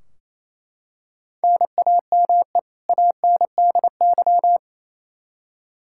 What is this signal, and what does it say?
Morse code, heard as a keyed tone, about 28 wpm, 700 Hz. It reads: NAME ANDY
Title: Practice QSO - Exchange 1 - Beginning - 28wpm